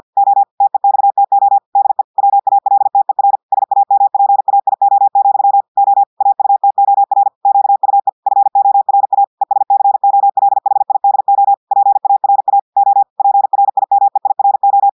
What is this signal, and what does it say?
Morse code, heard as a keyed tone, about 50 wpm, 800 Hz. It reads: O TEXTO DE WALTER SAMPAIO, O AUTOR QUE FORA ESCOLHIDO PARA O PRIMEIRO
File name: audios / portugais / portugais texte en clair 1 vitesse 50 mots minutes